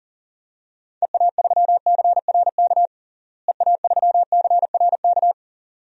Morse code, heard as a keyed tone, about 37 wpm, 700 Hz. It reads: EA3CRK EA3CRK
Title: DXCC-100 Call Signs - 6 Characters - Repeated 2x - 40wpm